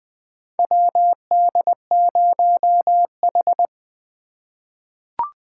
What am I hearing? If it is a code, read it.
Morse code, heard as a keyed tone, about 20 wpm, 700 Hz. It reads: WD0H
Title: USA Call Signs - 4 Characters - 4x Character Spacing - 20wpm